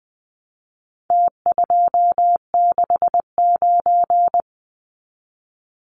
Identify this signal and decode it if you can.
Morse code, heard as a keyed tone, about 20 wpm, 700 Hz. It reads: T269
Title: Numbers - Contest Style - 4-Digit 20wpm